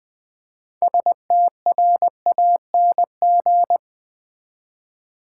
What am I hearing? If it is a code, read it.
Morse code, heard as a keyed tone, about 20 wpm, 700 Hz. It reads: STRANG
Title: Build A Word - 3 Letters At A Time - 8-Letter Words - 20wpm